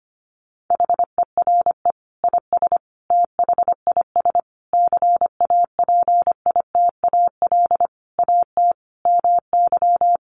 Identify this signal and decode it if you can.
Morse code, heard as a keyed tone, about 25 wpm, 700 Hz. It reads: HERE IS THIS CAPITAL AT MY
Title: Sentences from Top 1000 Words 25wpm